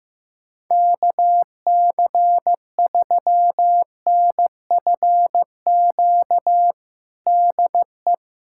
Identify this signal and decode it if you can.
Morse code, heard as a keyed tone, about 15 wpm, 700 Hz. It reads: KC3NFQ DE
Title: USA Callsign DE Callsign - 6 Characters - 15wpm